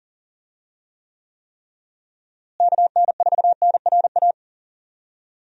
Morse code, heard as a keyed tone, about 39 wpm, 700 Hz. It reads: KN4NRA
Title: USA Call Signs - 6 Characters - Repeated 3x - 40wpm